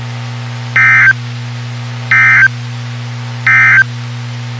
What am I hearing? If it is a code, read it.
Emergency Alert System (SAME) tones, heard as harsh buzzing bursts. It is heard from 0.6 to 3.9 s.
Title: WXK91 NOAA Weather Radio